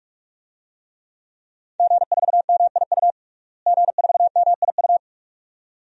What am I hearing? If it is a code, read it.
Morse code, heard as a keyed tone, about 46 wpm, 700 Hz. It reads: G4MIU G4MIU
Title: DXCC-100 Call Signs - 5 Characters - Repeated 2x - 45wpm